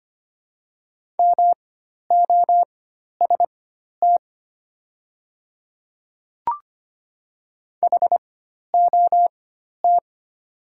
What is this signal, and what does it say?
Morse code, heard as a keyed tone, about 25 wpm, spaced as for 12 wpm, 700 Hz. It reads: MOST HOT
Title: 4x Character Spacing - Top 100 Words - 25wpm